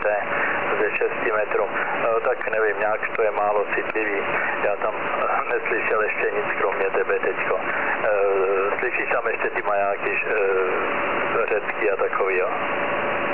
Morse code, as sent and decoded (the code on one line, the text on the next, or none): none